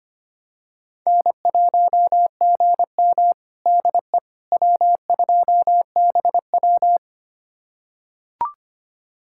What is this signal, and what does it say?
Morse code, heard as a keyed tone, about 25 wpm, 700 Hz. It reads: N1GM DE W2BW
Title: USA Callsign DE Callsign - 4 Characters - 25wpm